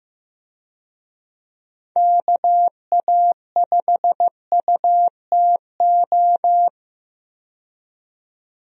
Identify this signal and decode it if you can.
Morse code, heard as a keyed tone, about 15 wpm, 700 Hz. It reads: KA5UTO
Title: USA Call Signs - 6 Characters - 15wpm